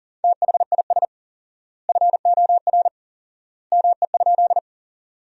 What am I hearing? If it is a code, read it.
Morse code, heard as a keyed tone, about 40 wpm, 700 Hz. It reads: THIS FOR ME?
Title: 4x Word-Spacing - Sentences from Top 200 Words - 40wpm